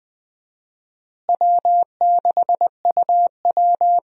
Morse code, heard as a keyed tone, about 20 wpm, 700 Hz. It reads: W6UW
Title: USA Call Signs - 4 Characters - Repeated 2x - 20wpm